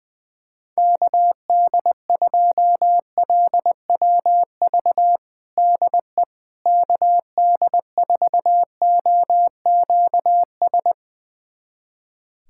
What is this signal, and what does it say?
Morse code, heard as a keyed tone, about 20 wpm, 700 Hz. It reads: KD2LWV DE KD4OQS
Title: USA Callsign DE Callsign - 6 Characters - 20wpm